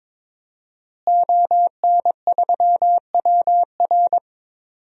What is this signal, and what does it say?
Morse code, heard as a keyed tone, about 22 wpm, 700 Hz. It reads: ON3WR
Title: DXCC-100 Call Signs - 5 Characters - Repeated 3x - 22wpm